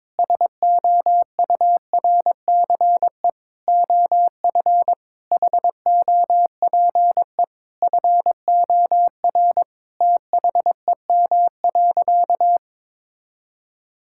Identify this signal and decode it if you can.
Morse code, heard as a keyed tone, about 22 wpm, 700 Hz. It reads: SOURCE OF HOPE FOR THEM.
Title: Sentences from Top 1000 Words - Review Part 5 - 22wpm